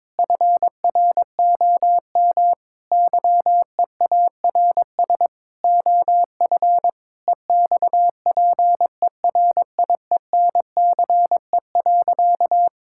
Morse code, heard as a keyed tone, about 22 wpm, 700 Hz. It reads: FROM YEARS OF EXPERIENCE.
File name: Difficult Sentences with Hints - 22wpm